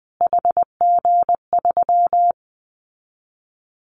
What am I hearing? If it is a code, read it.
Morse code, heard as a keyed tone, about 19 wpm, 700 Hz. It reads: HG3
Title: Callsign Prefixes for Top 25 DXCC - Repeated 3x - 20wpm